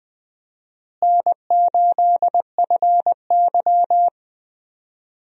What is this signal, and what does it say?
Morse code, heard as a keyed tone, about 20 wpm, 700 Hz. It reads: N8FY
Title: USA Call Signs - 4 Characters - Repeated 3x - 20wpm